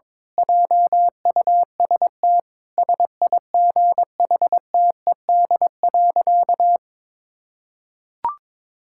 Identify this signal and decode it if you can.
Morse code, heard as a keyed tone, about 22 wpm, 700 Hz. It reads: JUST SIGHTED.
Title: Pangram Challenge v2 - 22wpm